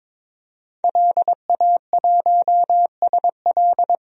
Morse code, heard as a keyed tone, about 22 wpm, 700 Hz. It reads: LA1SL
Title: DXCC-100 Call Signs - 5 Characters - 22wpm